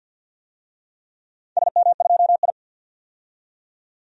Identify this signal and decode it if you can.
Morse code, heard as a keyed tone, about 47 wpm, 700 Hz. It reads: SM2I
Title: DXCC-100 Call Signs - 4 Characters - 50wpm